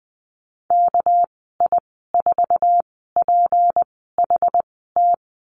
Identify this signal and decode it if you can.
Morse code, heard as a keyed tone, about 20 wpm, 700 Hz. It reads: K I 4 P H T
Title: 238 Lesson 38g - Letter q - Call Signs 20wpm